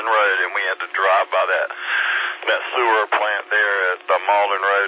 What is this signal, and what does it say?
Emergency Alert System (SAME) tones, heard as harsh buzzing bursts, with speech. It is heard from 0.3 to 4.0 s.